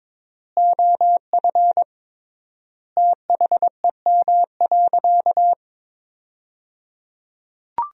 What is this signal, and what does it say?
Morse code, heard as a keyed tone, about 22 wpm, 700 Hz. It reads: OF THEM.
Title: 3x Word-Spacing - Sentences from Top 300 Words - 22wpm